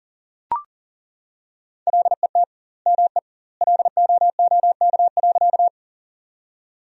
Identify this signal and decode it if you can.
Morse code, heard as a keyed tone, about 38 wpm, 700 Hz. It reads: LET ME LOOK.
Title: 2x Word-Spacing - Sentences from Top 300 Words - 40wpm